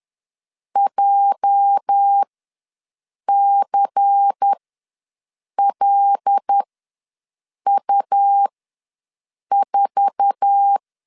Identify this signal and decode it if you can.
Morse code, heard as a keyed tone, about 10 wpm, spaced as for 6 wpm, 800 Hz. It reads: JCLU4
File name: LEZIONI / 10 SIGNS / 10A SLASH ws 10wpm 4121